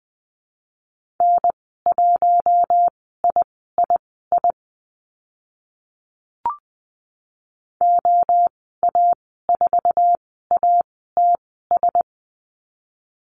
Morse code, heard as a keyed tone, about 20 wpm, spaced as for 14 wpm, 700 Hz. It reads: N1III OA4ATS
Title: 069 Lesson 12g - Letter d - Call Signs 20wpm